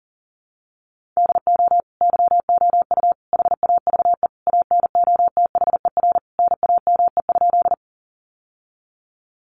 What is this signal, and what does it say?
Morse code, heard as a keyed tone, about 40 wpm, 700 Hz. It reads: DO YOU HAVE ANOTHER NAME?
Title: Sentences from Top 600 Words 40wpm